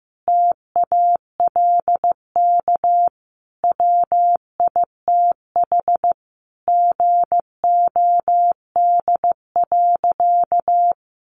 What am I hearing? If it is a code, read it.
Morse code, heard as a keyed tone, about 15 wpm, 700 Hz. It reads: TALK WITH GOD.
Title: A Sound Of Thunder 15wpm